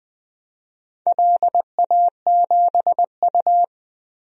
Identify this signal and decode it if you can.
Morse code, heard as a keyed tone, about 20 wpm, 700 Hz. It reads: LA7U
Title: DXCC-100 Call Signs - 4 Characters - 20wpm